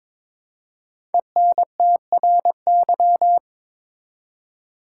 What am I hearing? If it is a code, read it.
Morse code, heard as a keyed tone, about 22 wpm, 700 Hz. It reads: ENTRY
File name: 5 Letter Words - First 3 Letters Given - 22wpm